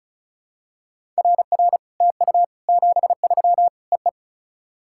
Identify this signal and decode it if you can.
Morse code, heard as a keyed tone, about 34 wpm, 700 Hz. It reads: RR TU 73 EE
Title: SOTA QSO Practice 35wpm